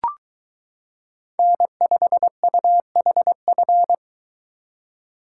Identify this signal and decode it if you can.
Morse code, heard as a keyed tone, about 23 wpm, 700 Hz. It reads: N5UHF